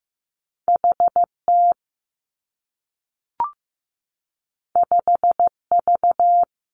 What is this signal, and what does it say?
Morse code, heard as a keyed tone, about 15 wpm, 700 Hz. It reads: HT 5V
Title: Two Challenging Letter-Number Combinations 15wpm